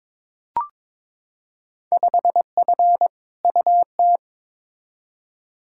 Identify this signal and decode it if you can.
Morse code, heard as a keyed tone, about 22 wpm, 700 Hz. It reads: 5F UT
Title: Field Day Exchanges - 22wpm